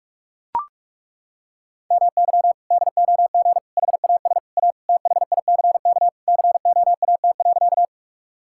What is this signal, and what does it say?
Morse code, heard as a keyed tone, about 41 wpm, 700 Hz. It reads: MY DOG HAS A THICK COAT.
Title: Sentences from Top 1600 Words 45wpm